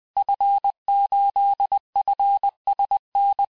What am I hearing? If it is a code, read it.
Morse code, heard as a keyed tone, about 20 wpm, 800 Hz. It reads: F8FSN
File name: cours-du-mercredi / audios / cours-858-Fin-de-QSO-vitesse-19mpm